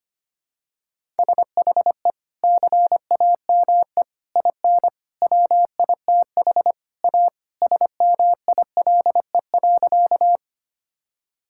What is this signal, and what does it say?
Morse code, heard as a keyed tone, about 25 wpm, 700 Hz. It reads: SHE CAME IN WITH A SMILE.
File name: practice / 25wpm / Sentences with 2-Word Phrases - 25wpm